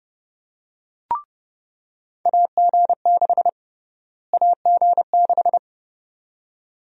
Morse code, heard as a keyed tone, about 31 wpm, 700 Hz. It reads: AG6 AG6
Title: USA Call Sign Prefixes - Repeated 2x - 30wpm